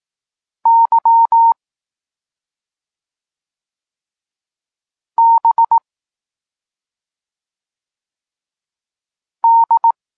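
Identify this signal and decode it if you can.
Morse code, heard as a keyed tone, about 18 wpm, 920 Hz. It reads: Y B D
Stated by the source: Polish Morse lesson teaching the letter C, from SP9MOA Niepołomicki Klub Krótkofalowców.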